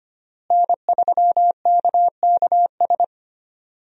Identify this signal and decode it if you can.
Morse code, heard as a keyed tone, about 25 wpm, 700 Hz. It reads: N3KKS
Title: USA Call Signs - 5 Characters - Repeated 3x - 25wpm